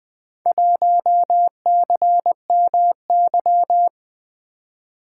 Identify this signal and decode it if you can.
Morse code, heard as a keyed tone, about 20 wpm, 700 Hz. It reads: 1CMY
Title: Four Letter-Number Combinations - 20wpm